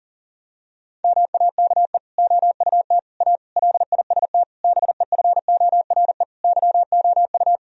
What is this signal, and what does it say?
Morse code, heard as a keyed tone, about 40 wpm, 700 Hz. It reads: MAKE OUT A LIST BEFORE YOU
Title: Sentences from Top 1000 Words - Review Part 2 - 40wpm